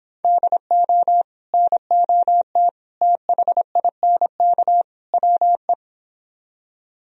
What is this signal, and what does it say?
Morse code, heard as a keyed tone, about 26 wpm, 700 Hz. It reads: DO NOT THINK WE